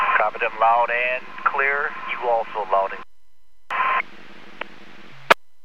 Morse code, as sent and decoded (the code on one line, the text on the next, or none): none